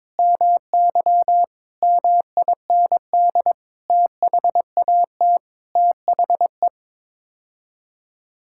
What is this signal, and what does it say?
Morse code, heard as a keyed tone, about 22 wpm, 700 Hz. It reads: MY MIND THAT THE
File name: Sets of 9 Words - 22wpm